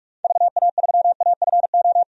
Morse code, heard as a keyed tone, about 42 wpm, 700 Hz. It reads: VA3AFO
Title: DXCC-100 Call Signs - 6 Characters - 45wpm